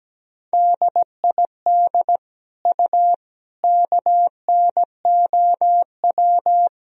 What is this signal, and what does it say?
Morse code, heard as a keyed tone, about 17 wpm, 700 Hz. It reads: DID U KNOW
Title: Sentences from Top 100 Words and Abbreviations 17wpm